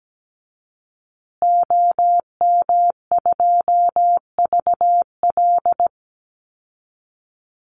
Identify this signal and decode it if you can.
Morse code, heard as a keyed tone, about 17 wpm, 700 Hz. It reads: OM2VL
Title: Call Signs Repeated 3x - 17wpm